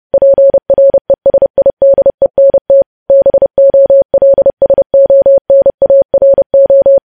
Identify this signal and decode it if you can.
Morse code, heard as a keyed tone, about 30 wpm, 550 Hz. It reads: PRESIDENT BOLSONARO
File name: podcast / News headlines-1x-30x30